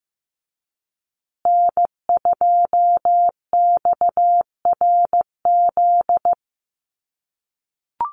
Morse code, heard as a keyed tone, about 15 wpm, 700 Hz. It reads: N2XRZ
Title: USA Call Signs Repeated 3x 15wpm